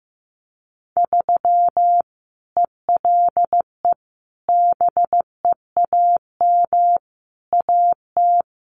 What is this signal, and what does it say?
Morse code, heard as a keyed tone, about 15 wpm, 700 Hz. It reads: 3 ELE BEAM AT
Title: Common QSO Phrases 15wpm